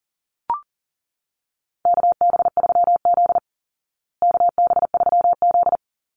Morse code, heard as a keyed tone, about 40 wpm, 700 Hz. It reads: KB3Z KB3Z
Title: Call Signs Repeated 2x - 40wpm